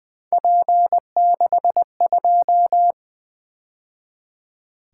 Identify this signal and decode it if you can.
Morse code, heard as a keyed tone, about 25 wpm, 700 Hz. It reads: P62
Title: Speed-Racing - Three Letter-Number Combinations - 20wpm